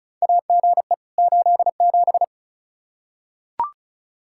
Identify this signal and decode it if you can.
Morse code, heard as a keyed tone, about 34 wpm, 700 Hz. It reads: AGE 87
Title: K1USN SST QSO Practice 35wpm